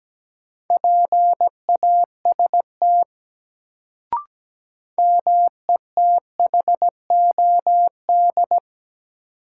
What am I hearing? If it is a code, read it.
Morse code, heard as a keyed tone, about 17 wpm, 700 Hz. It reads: PAST METHOD
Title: Sentences from Top 600 Words - New Word Drill - 17wpm